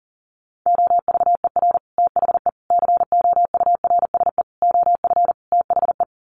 Morse code, heard as a keyed tone, about 40 wpm, 700 Hz. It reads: OVER THE COURSE OF THE
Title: Sets of 5 Words 40wpm